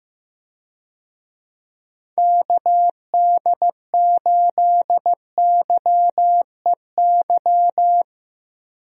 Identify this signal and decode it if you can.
Morse code, heard as a keyed tone, about 15 wpm, 700 Hz. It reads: KD8YEY
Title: USA Call Signs - 6 Characters - Repeated 2x - 15wpm